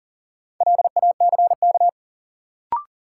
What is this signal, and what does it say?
Morse code, heard as a keyed tone, about 40 wpm, 700 Hz. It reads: LACK
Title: Sets of 3 Words - New Word Drill - 40wpm